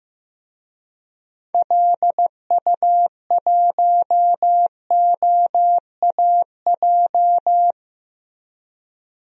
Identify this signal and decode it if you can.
Morse code, heard as a keyed tone, about 15 wpm, 700 Hz. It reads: LU1OAJ
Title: DXCC-100 Call Signs - 6 Characters - Repeated 2x - 15wpm